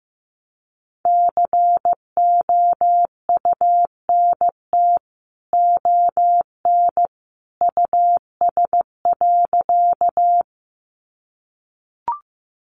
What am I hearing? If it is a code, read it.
Morse code, heard as a keyed tone, about 15 wpm, 700 Hz. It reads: COUNT ON US.
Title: Sentences from Top 1500 Words 15wpm